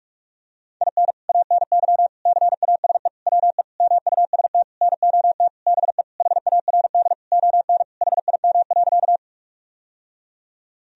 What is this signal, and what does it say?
Morse code, heard as a keyed tone, about 41 wpm, 700 Hz. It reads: IN ANY CASE WE MUST NOT BE HARD ON HIM.
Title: Sentences from Top 1000 Words - Review Part 1 - 45wpm